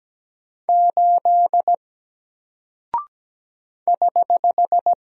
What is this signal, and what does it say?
Morse code, heard as a keyed tone, about 17 wpm, 700 Hz. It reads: 8 <HH>
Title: Single Character (Letter:Number:Punctuation:Prosign) Alt 17wpm